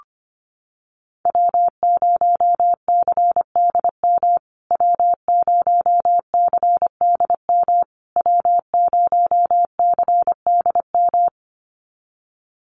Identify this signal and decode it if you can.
Morse code, heard as a keyed tone, about 25 wpm, 700 Hz. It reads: W0CDM W0CDM W0CDM
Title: USA Call Signs Repeated 3x 25wpm